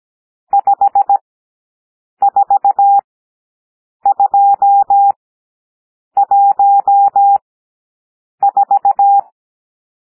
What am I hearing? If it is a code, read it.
Morse code, heard as a keyed tone, about 17 wpm, spaced as for 7 wpm, 800 Hz. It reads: 54214
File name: SOUND NUMBERS 17 wpm